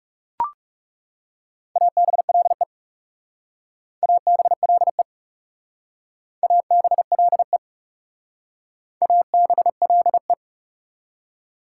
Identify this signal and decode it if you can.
Morse code, heard as a keyed tone, about 39 wpm, 700 Hz. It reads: ABLE ABLE ABLE ABLE